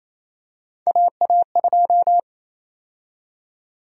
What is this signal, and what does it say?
Morse code, heard as a keyed tone, about 28 wpm, 700 Hz. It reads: AA2
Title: USA Call Sign Prefixes - 28wpm